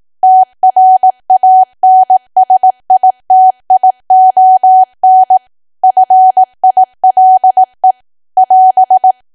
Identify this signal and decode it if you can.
Morse code, heard as a keyed tone, about 18 wpm, 750 Hz. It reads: TRANSITION FILE <AS>
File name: Archive / 18 WPM / 18WPM